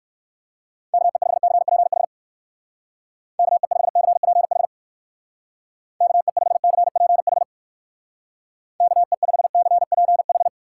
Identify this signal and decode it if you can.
Morse code, heard as a keyed tone, about 58 wpm, 700 Hz. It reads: KE5CPH KE5CPH KE5CPH KE5CPH